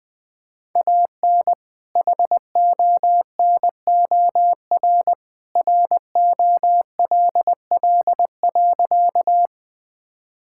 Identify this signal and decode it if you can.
Morse code, heard as a keyed tone, about 20 wpm, 700 Hz. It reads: AN HONOR ROLL.
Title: All-Age Appropriate Jokes - 20wpm